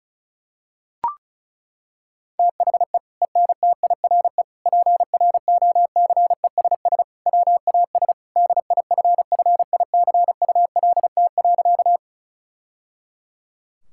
Morse code, heard as a keyed tone, about 36 wpm, 700 Hz. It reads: THE ENTIRE PROCESS WAS DIFFICULT.
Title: Sentences from Top 1000 Words - Review Part 5 - 35wpm